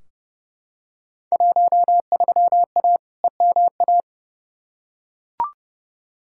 Morse code, heard as a keyed tone, about 30 wpm, 700 Hz. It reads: 13A EMA
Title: Field Day Exchanges Encore - 30wpm